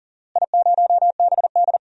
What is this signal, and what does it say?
Morse code, heard as a keyed tone, about 40 wpm, 700 Hz. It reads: I0BD
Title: DXCC-100 Call Signs - 4 Characters - Repeated 2x - 40wpm